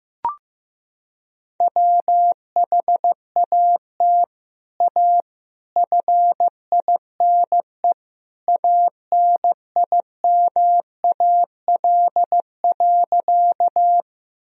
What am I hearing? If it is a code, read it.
Morse code, heard as a keyed tone, about 15 wpm, 700 Hz. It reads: WHAT A FINE ANIMAL.
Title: Sentences from Top 1000 Words - Review Part 1 - 15wpm